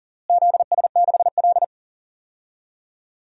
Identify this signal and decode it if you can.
Morse code, heard as a keyed tone, about 40 wpm, 700 Hz. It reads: ZS6L
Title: DXCC-100 Call Signs - 4 Characters - Repeated 2x - 40wpm